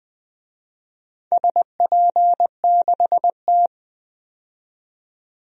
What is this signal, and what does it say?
Morse code, heard as a keyed tone, about 20 wpm, 700 Hz. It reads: SP6T
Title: DXCC-100 Call Signs - 4 Characters - 20wpm